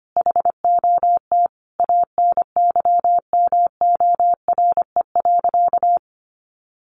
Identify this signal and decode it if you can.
Morse code, heard as a keyed tone, about 25 wpm, 700 Hz. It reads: HOT ANYMORE.
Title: Easy Phrases 25wpm